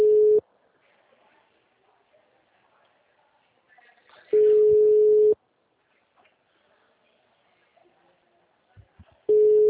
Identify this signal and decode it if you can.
Ringback tone (Europe, 425 Hz).
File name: out- -2002- - - .2312